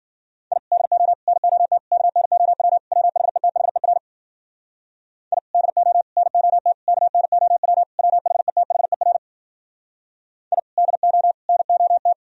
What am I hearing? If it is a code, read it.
Morse code, heard as a keyed tone, about 56 wpm, 700 Hz. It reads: I DO NOT KNOW WHETHER I DO NOT KNOW WHETHER I DO NOT